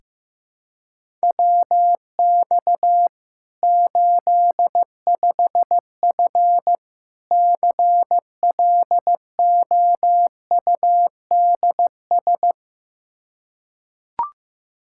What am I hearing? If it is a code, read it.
Morse code, heard as a keyed tone, about 16 wpm, 700 Hz. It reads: WX 85F CLOUDS
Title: Practice QSO - Exchange 2 - Info - 15wpm